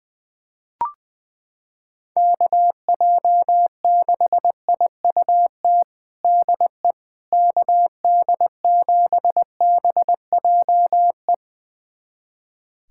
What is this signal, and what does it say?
Morse code, heard as a keyed tone, about 21 wpm, 700 Hz. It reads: KJ6IUT DE KD7BJE
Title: USA Callsign DE Callsign 20wpm